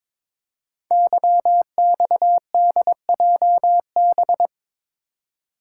Morse code, heard as a keyed tone, about 22 wpm, 700 Hz. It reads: YXDJB
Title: Code Groups - 5 Letters - 22wpm